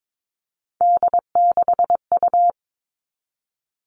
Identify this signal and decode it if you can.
Morse code, heard as a keyed tone, about 22 wpm, 700 Hz. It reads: D6U
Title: All Cut Numbers - 3-Digit 22wpm